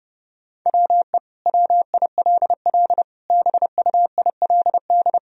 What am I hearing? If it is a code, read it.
Morse code, heard as a keyed tone, about 32 wpm, 700 Hz. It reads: WE WILL BUILD
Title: Sentences from Top 1000 Words - Review Part 1 - 30wpm